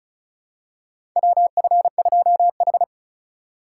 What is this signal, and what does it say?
Morse code, heard as a keyed tone, about 36 wpm, 700 Hz. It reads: WF2H
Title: USA Call Signs - 4 Characters - 3x Character Spacing - 35wpm